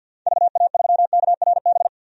Morse code, heard as a keyed tone, about 49 wpm, 700 Hz. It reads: VA3KRB
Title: DXCC-100 Call Signs - 6 Characters - 50wpm